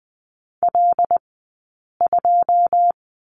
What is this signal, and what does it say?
Morse code, heard as a keyed tone, about 20 wpm, 700 Hz. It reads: L 2
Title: 086 Lesson 15d - Number 5 - Two Characters 20wpm